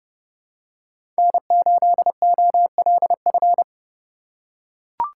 Morse code, heard as a keyed tone, about 30 wpm, 700 Hz. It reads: N8OLF
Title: USA Call Signs - 5 Characters - 2x Character Spacing - 30wpm